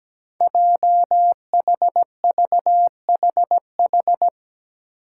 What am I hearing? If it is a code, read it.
Morse code, heard as a keyed tone, about 17 wpm, 700 Hz. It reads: JHVHH
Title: Five Challenging Letter-Number Combinations 17wpm